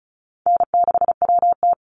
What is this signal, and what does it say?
Morse code, heard as a keyed tone, about 36 wpm, 700 Hz. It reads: N6WT
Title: NA SOTA Chaser Callsigns 35wpm